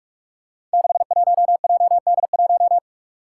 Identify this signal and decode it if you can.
Morse code, heard as a keyed tone, about 46 wpm, 700 Hz. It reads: 61JD1
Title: Five Challenging Letter-Number Combinations 45wpm